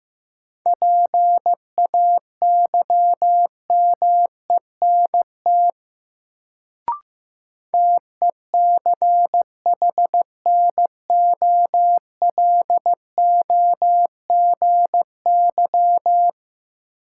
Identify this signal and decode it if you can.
Morse code, heard as a keyed tone, about 15 wpm, 700 Hz. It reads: PAYMENT TECHNOLOGY
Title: Sentences from Top 1200 Words - New Word Drill Alt - 15wpm